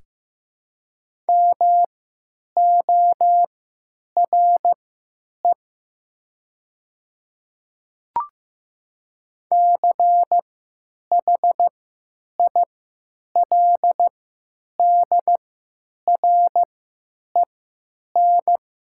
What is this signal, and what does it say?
Morse code, heard as a keyed tone, about 15 wpm, spaced as for 9 wpm, 700 Hz. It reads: MORE CHILDREN
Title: 3x Character Spacing - Top 300 Words - 15wpm